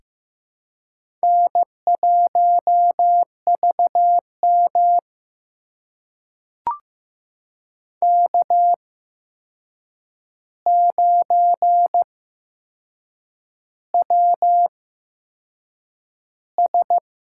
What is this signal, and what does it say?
Morse code, heard as a keyed tone, about 15 wpm, spaced as for 6 wpm, 700 Hz. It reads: N1VM K9WS
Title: USA Call Signs - 4 Characters - 8x Character Spacing - 15wpm